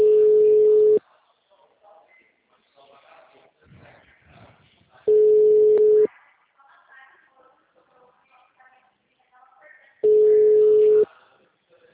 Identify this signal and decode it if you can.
Ringback tone (Europe, 425 Hz), with speech.